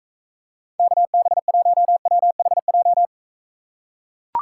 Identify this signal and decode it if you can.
Morse code, heard as a keyed tone, about 44 wpm, 700 Hz. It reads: KB1WHJ